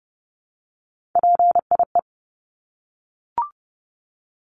Morse code, heard as a keyed tone, about 30 wpm, 700 Hz. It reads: PIE
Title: Top 1751-2000 Words 30wpm